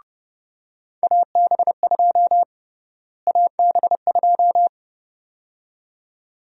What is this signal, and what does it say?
Morse code, heard as a keyed tone, about 30 wpm, 700 Hz. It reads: AB2 AB2
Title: USA Call Sign Prefixes - Repeated 2x - 30wpm